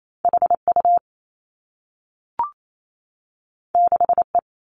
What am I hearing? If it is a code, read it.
Morse code, heard as a keyed tone, about 28 wpm, 700 Hz. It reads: H U 6 E
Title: Two Challenging Letter-Number Combinations 28wpm